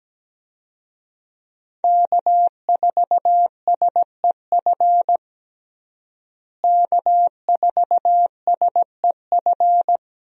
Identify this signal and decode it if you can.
Morse code, heard as a keyed tone, about 17 wpm, 700 Hz. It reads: K4SEF K4SEF
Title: USA Call Signs - 5 Characters - Repeated 2x - 17wpm